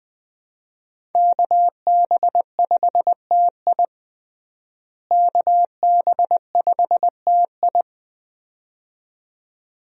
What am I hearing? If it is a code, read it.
Morse code, heard as a keyed tone, about 20 wpm, 700 Hz. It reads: KB5TI KB5TI
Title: USA Call Signs - 5 Characters - Repeated 2x - 20wpm